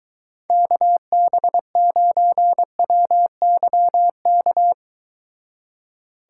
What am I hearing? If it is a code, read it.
Morse code, heard as a keyed tone, about 23 wpm, 700 Hz. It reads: KB9WYK